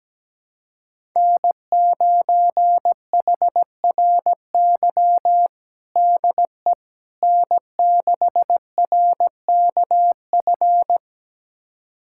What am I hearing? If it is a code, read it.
Morse code, heard as a keyed tone, about 17 wpm, 700 Hz. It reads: N9HRY DE N6RKF
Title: USA Callsign DE Callsign - 5 Characters - 17wpm